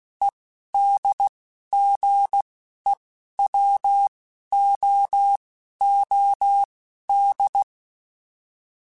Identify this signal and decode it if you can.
Morse code, heard as a keyed tone, about 16 wpm, spaced as for 11 wpm, 800 Hz. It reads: EDGEWOOD